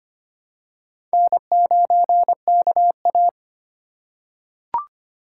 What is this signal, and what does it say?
Morse code, heard as a keyed tone, about 25 wpm, 700 Hz. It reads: N9KA
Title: USA Call Signs - 3x Character Spacing - 25wpm